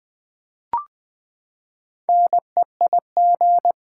Morse code, heard as a keyed tone, about 20 wpm, 700 Hz. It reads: NEIG
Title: Build A Word - 4 Letters At A Time - 11-Letter Words - 20wpm